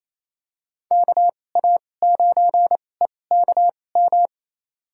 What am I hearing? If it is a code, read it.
Morse code, heard as a keyed tone, about 28 wpm, spaced as for 20 wpm, 700 Hz. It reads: KA9EKM
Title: USA Call Signs - 6 Characters - 2x Character Spacing - 28wpm